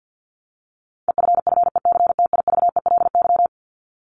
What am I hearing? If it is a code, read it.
Morse code, heard as a keyed tone, about 50 wpm, 700 Hz. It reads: EFFECTIVELY
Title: Top 1251-1500 Words 50wpm